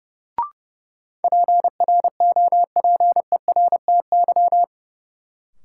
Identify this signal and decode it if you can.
Morse code, heard as a keyed tone, about 30 wpm, 700 Hz. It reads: PROPERTY
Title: Sentences from Top 1000 Words - New Word Drill - 30wpm